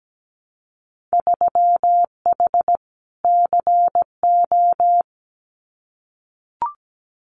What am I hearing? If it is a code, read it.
Morse code, heard as a keyed tone, about 17 wpm, 700 Hz. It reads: 3H CO
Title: Winter Field Day Exchanges 17wpm